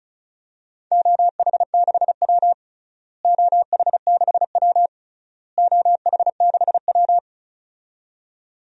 Morse code, heard as a keyed tone, about 36 wpm, 700 Hz. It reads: OH6W OH6W OH6W
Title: DXCC-100 Call Signs - 4 Characters - Repeated 3x - 35wpm